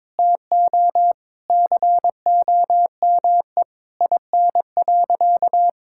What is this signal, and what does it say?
Morse code, heard as a keyed tone, about 21 wpm, 700 Hz. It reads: TO COME IN.
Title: Sentences from Top 1000 Words - Review Part 5 - 22wpm